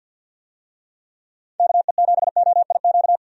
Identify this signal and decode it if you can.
Morse code, heard as a keyed tone, about 50 wpm, 700 Hz. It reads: KE7OIQ
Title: USA Call Signs - 6 Characters - 50wpm